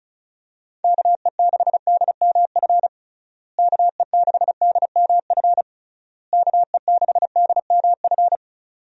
Morse code, heard as a keyed tone, about 34 wpm, 700 Hz. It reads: KE6DMF KE6DMF KE6DMF
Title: USA Call Signs - 6 Characters - Repeated 3x - 35wpm